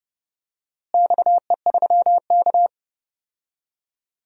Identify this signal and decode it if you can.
Morse code, heard as a keyed tone, about 32 wpm, 700 Hz. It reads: XE3K
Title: DXCC-100 Call Signs - 4 Characters - Repeated 2x - 30wpm